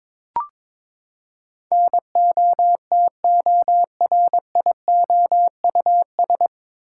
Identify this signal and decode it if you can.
Morse code, heard as a keyed tone, about 21 wpm, 700 Hz. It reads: NOTORIOUS
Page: Morse Code Ninja practice 22wpm